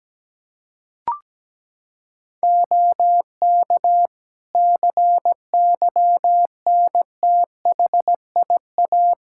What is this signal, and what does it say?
Morse code, heard as a keyed tone, about 17 wpm, 700 Hz. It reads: OK CYNTHIA
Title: K1USN SST QSO Practice 17wpm